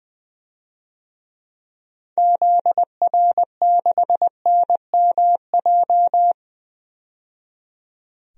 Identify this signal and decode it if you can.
Morse code, heard as a keyed tone, about 20 wpm, 700 Hz. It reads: ZR6NMJ
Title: DXCC-100 Call Signs - 6 Characters - 20wpm